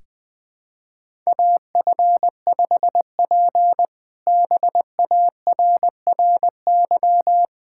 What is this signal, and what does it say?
Morse code, heard as a keyed tone, about 20 wpm, 700 Hz. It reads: AF5P BARRY
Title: K1USN SST QSO Practice - 20wpm